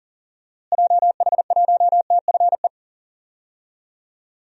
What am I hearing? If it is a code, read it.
Morse code, heard as a keyed tone, about 40 wpm, 700 Hz. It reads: JH1TFE
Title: DXCC-100 Call Signs - 6 Characters - Repeated 2x - 40wpm